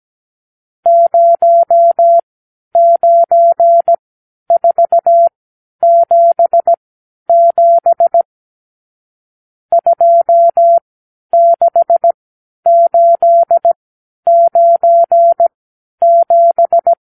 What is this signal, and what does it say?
Morse code, heard as a keyed tone, about 17 wpm, spaced as for 10 wpm, 680 Hz. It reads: 09477 26897
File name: arrl / Review (1-0)